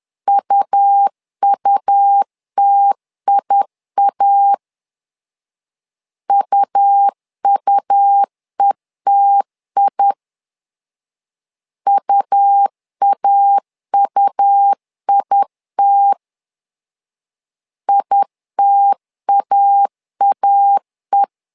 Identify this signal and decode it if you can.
Morse code, heard as a keyed tone, about 10 wpm, 800 Hz. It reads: UUTIA UUETI UAUIT ITAAE
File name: LEZIONI / 2 UHDN / 2 U 10wpm 1622